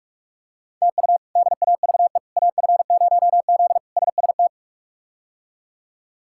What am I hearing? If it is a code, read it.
Morse code, heard as a keyed tone, about 44 wpm, 700 Hz. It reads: TU DAVE AF0Z SST
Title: K1USN SST QSO Practice - 45wpm